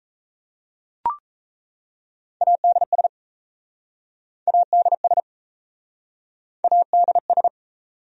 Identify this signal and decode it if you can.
Morse code, heard as a keyed tone, about 38 wpm, 700 Hz. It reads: ADS ADS ADS